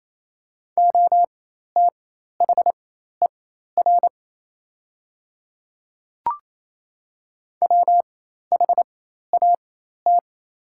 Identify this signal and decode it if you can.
Morse code, heard as a keyed tone, about 28 wpm, spaced as for 13 wpm, 700 Hz. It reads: OTHER WHAT
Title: 4x Character Spacing - Top 500 Words - 28wpm